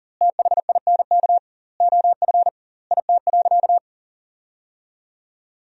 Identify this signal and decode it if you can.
Morse code, heard as a keyed tone, about 40 wpm, 700 Hz. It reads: THINK OF IT.
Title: 2x Word-Spacing - Sentences from Top 200 Words - 40wpm